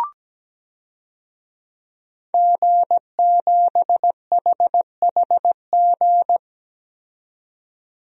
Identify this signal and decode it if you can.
Morse code, heard as a keyed tone, about 17 wpm, 700 Hz. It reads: G7HHG
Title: DXCC-100 Call Signs - 5 Characters - 17wpm